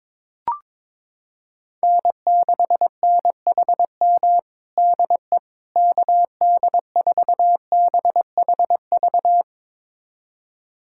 Morse code, heard as a keyed tone, about 22 wpm, 700 Hz. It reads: N6NHM DE KD4BHV
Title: USA Callsign DE Callsign 22wpm